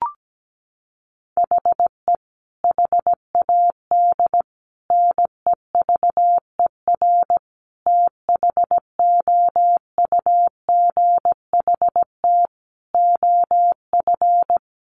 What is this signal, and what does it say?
Morse code, heard as a keyed tone, about 17 wpm, 700 Hz. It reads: HE HAD NEVER THOUGHT OF
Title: Sentences from Top 1300 Words 17wpm